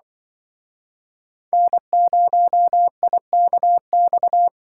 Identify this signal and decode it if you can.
Morse code, heard as a keyed tone, about 23 wpm, 700 Hz. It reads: N0IKX
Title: Speed Racing - USA 1x3 Call Signs 20wpm